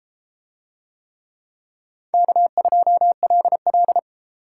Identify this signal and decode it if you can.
Morse code, heard as a keyed tone, about 33 wpm, 700 Hz. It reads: K2LL